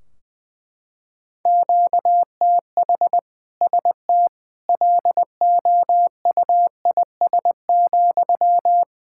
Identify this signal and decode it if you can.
Morse code, heard as a keyed tone, about 20 wpm, 700 Hz. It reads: QTH ST LOUIS,
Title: Practice QSO - Exchange 1 - Beginning - 20wpm